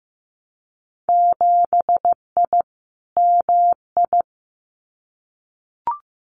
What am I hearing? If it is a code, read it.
Morse code, heard as a keyed tone, about 15 wpm, 700 Hz. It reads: 7I MI
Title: Winter Field Day Exchanges Encore 15wpm